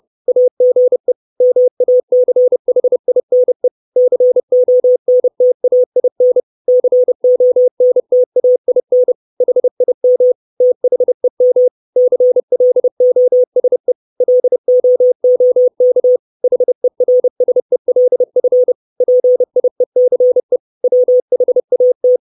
Morse code, heard as a keyed tone, about 30 wpm, 500 Hz. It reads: AGE MACHINE CONTAIN CONTAIN HIM THEM CLOSE LOOK HERSELF PIECE WHAT
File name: 632 Words 30WPM